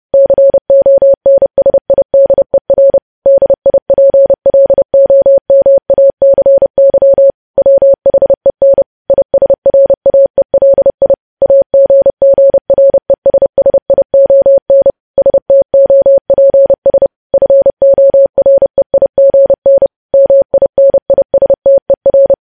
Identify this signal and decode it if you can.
Morse code, heard as a keyed tone, about 30 wpm, 550 Hz. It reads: CONSIDER DIPLOMACY WHEN ISRAELI AGGRESSION STOPS FOREIGN MINISTER
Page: news headlines in morse code at 30 wpm